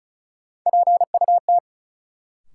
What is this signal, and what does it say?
Morse code, heard as a keyed tone, about 36 wpm, 700 Hz. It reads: PUT
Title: Sentences from Top 200 Words - New Word Drill - 35wpm